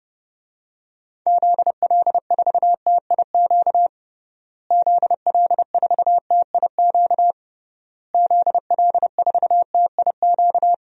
Morse code, heard as a keyed tone, about 30 wpm, 700 Hz. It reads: ZL4TIQ ZL4TIQ ZL4TIQ
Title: DXCC-100 Call Signs - 6 Characters - Repeated 3x - 30wpm